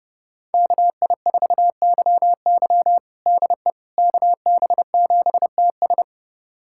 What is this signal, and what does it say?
Morse code, heard as a keyed tone, about 32 wpm, 700 Hz. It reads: KI4YY DE KB7TS
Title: USA Callsign DE Callsign - 5 Characters - 30wpm